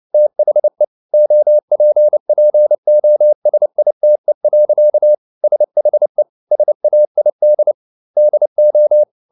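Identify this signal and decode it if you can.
Morse code, heard as a keyed tone, about 29 wpm, 600 Hz. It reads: THE OPPOSITE. SHE SAID DO
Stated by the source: Morse code at 29 wpm, sending Ken Follett's novel Never in English, from CW med Gnister.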